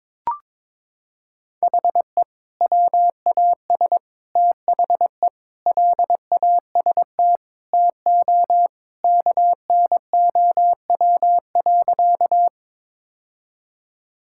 Morse code, heard as a keyed tone, about 22 wpm, 700 Hz. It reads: HE WAS THE LAST TO KNOW.
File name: practice / 22wpm / Sentences with 2-Word Phrases - 22wpm